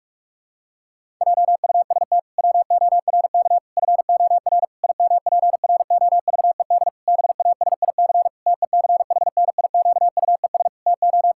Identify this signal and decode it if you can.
Morse code, heard as a keyed tone, about 46 wpm, 700 Hz. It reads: JUST WORK FOR IMPROVED BASIC TECHNIQUES TO
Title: Pangram Challenge v2 - 45wpm